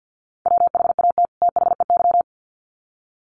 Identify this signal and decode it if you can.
Morse code, heard as a keyed tone, about 50 wpm, 700 Hz. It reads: WHAT THEY
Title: Sets of 2 Words 50wpm